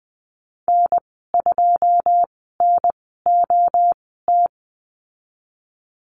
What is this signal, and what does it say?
Morse code, heard as a keyed tone, about 20 wpm, spaced as for 14 wpm, 700 Hz. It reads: N2NOT
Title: 128 Lesson 21g - Number 6 - Call Signs 20wpm